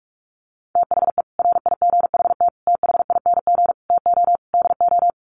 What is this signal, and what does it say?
Morse code, heard as a keyed tone, about 44 wpm, 700 Hz. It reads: THE RIGHT THING TO DO
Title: Sets of 5 Words 45wpm